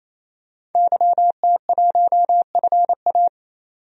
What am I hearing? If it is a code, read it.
Morse code, heard as a keyed tone, about 28 wpm, 700 Hz. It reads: YT1FA
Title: DXCC-100 Call Signs - 5 Characters - 28wpm